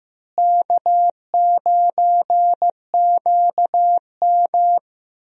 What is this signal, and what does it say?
Morse code, heard as a keyed tone, about 15 wpm, 700 Hz. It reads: K9QM
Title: USA Call Signs - 4 Characters - Repeated 2x - 15wpm